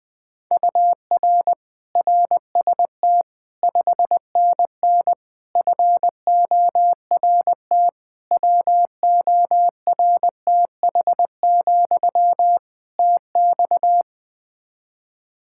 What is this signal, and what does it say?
Morse code, heard as a keyed tone, about 20 wpm, 700 Hz. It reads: UR RST 5NN FORT WORTH, TX
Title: Practice QSO - Exchange 1 - Beginning - 20wpm